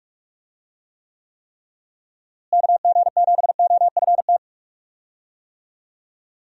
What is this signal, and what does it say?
Morse code, heard as a keyed tone, about 44 wpm, 700 Hz. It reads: KG7OFT
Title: USA Call Signs - 6 Characters - Repeated 3x - 45wpm